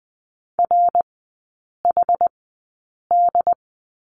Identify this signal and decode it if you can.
Morse code, heard as a keyed tone, about 20 wpm, spaced as for 8 wpm, 700 Hz. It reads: RHD
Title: 067 Lesson 12e - Letter d - Three Characters 20wpm